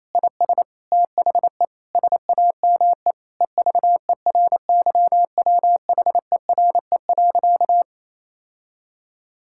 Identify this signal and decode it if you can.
Morse code, heard as a keyed tone, about 28 wpm, 700 Hz. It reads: IS THE SAME EVERYWHERE.
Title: Sentences from Top 1600 Words 28wpm